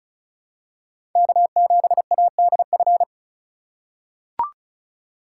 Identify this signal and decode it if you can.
Morse code, heard as a keyed tone, about 36 wpm, 700 Hz. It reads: K7ADF
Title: USA Call Signs - 5 Characters - 3x Character Spacing - 35wpm